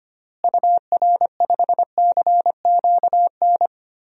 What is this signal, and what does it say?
Morse code, heard as a keyed tone, about 25 wpm, 700 Hz. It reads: UR5CQN
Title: DXCC-100 Call Signs - 6 Characters - Repeated 3x - 25wpm